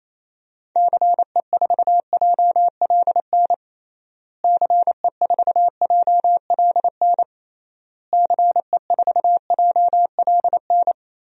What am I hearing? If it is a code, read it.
Morse code, heard as a keyed tone, about 28 wpm, 700 Hz. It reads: CE4JLN CE4JLN CE4JLN
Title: DXCC-100 Call Signs - 6 Characters - Repeated 3x - 28wpm